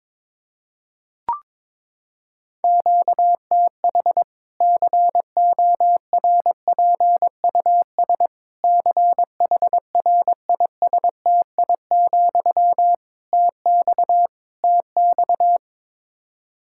Morse code, heard as a keyed tone, about 22 wpm, 700 Hz. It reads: QTH CORPUS CHRISTI, TX TX
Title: Practice QSO - Exchange 1 - Beginning - 22wpm